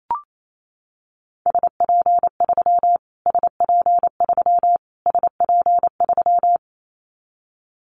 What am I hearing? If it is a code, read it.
Morse code, heard as a keyed tone, about 28 wpm, 700 Hz. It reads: SP3 SP3 SP3
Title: Callsign Prefixes for Top 25 DXCC - Repeated 3x - 28wpm